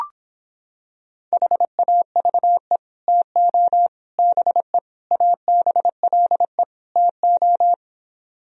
Morse code, heard as a keyed tone, about 26 wpm, 700 Hz. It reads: HAVE TO BE ABLE TO